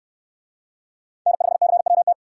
Morse code, heard as a keyed tone, about 69 wpm, 700 Hz. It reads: N5CWT